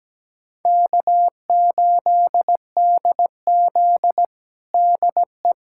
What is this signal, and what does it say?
Morse code, heard as a keyed tone, about 17 wpm, 700 Hz. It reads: K8DZ DE
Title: USA Callsign DE Callsign - 4 Characters - 17wpm